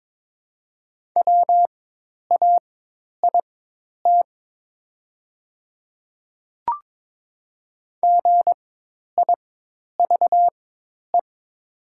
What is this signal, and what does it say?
Morse code, heard as a keyed tone, about 22 wpm, spaced as for 10 wpm, 700 Hz. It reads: WAIT GIVE
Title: 4x Character Spacing - Top 500 Words - 22wpm